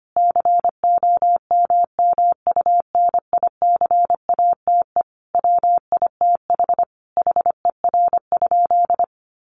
Morse code, heard as a keyed tone, about 25 wpm, 700 Hz. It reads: COMMUNICATE WITH HER?
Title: Sentences from Top 1300 Words 25wpm